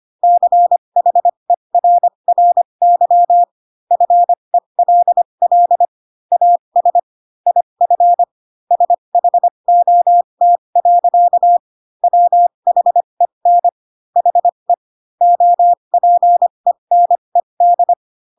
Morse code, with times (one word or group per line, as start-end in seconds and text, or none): 0.2-3.4 CHERRY
3.9-5.9 FELL
6.3-7.0 AS
7.5-8.2 IF
8.7-11.6 SHOT.
12.0-13.7 WHEN
14.2-14.7 HE
15.2-17.9 OPENED